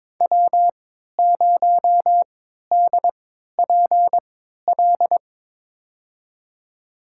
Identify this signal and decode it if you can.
Morse code, heard as a keyed tone, about 23 wpm, spaced as for 13 wpm, 700 Hz. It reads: W0DPL
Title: USA Call Signs - 5 Characters - 3x Character Spacing - 22wpm